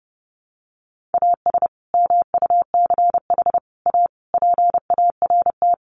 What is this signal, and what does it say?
Morse code, heard as a keyed tone, about 30 wpm, 700 Hz. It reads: AS MUCH A PART
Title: Sets of 5 Words Encore2 30wpm